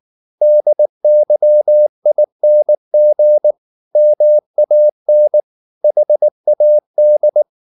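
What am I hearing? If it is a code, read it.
Morse code, heard as a keyed tone, about 19 wpm, 600 Hz. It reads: DYING MAN HAD
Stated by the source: Morse code at 19 wpm from CW med Gnister, reading Ken Follett's novel Never in English.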